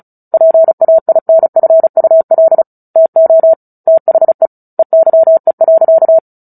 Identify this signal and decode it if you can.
Morse code, heard as a keyed tone, about 34 wpm, 650 Hz. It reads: PAINFUL TO THE EYE.
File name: Book Two / 09-Chapter 9 - 35 WPM